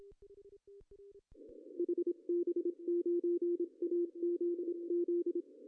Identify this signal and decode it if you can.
Morse code, heard as a keyed tone, about 26 wpm, 340 Hz, with radio noise. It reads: HB9AZZ
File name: CW / cqww-cw-k5tr